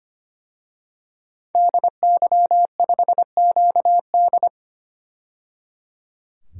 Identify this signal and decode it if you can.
Morse code, heard as a keyed tone, about 25 wpm, 700 Hz. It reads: DY5QD
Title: DXCC-100 Call Signs - 5 Characters - 25wpm